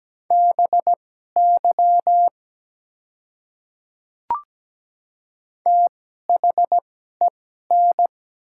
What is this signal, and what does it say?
Morse code, heard as a keyed tone, about 17 wpm, spaced as for 12 wpm, 700 Hz. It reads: BY THEN
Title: 2x Character Spacing - Top 100 Words - 17wpm